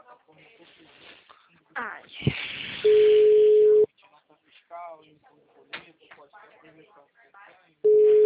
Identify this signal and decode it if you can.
Ringback tone (Europe, 425 Hz), with speech.